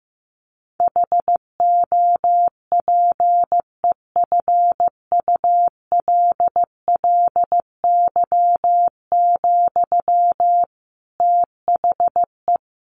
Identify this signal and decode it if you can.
Morse code, heard as a keyed tone, about 15 wpm, 700 Hz. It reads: HOPEFULLY, THE
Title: Sentences from Top 1500 Words 15wpm